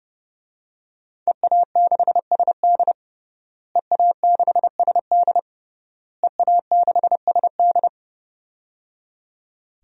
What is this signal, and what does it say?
Morse code, heard as a keyed tone, about 30 wpm, 700 Hz. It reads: EA6SD EA6SD EA6SD
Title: DXCC-100 Call Signs - 5 Characters - Repeated 3x - 30wpm